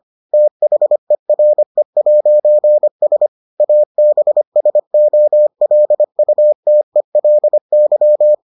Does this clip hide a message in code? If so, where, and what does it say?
Yes: Morse code from 0.3 to 8.3 s, heard as a keyed tone, about 25 wpm, 600 Hz. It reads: THERE'S ABSOLUTELY